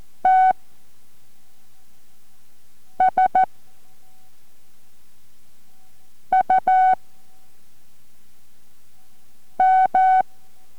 Morse code, heard as a keyed tone, about 14 wpm, 750 Hz. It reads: T S U M